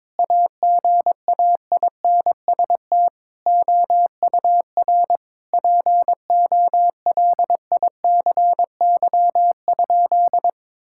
Morse code, heard as a keyed tone, about 22 wpm, 700 Hz. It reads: AGAINST OUR POLICY?
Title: Sentences from Top 1000 Words - Review Part 5 - 22wpm